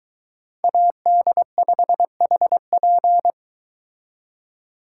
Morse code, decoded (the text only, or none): AD5HP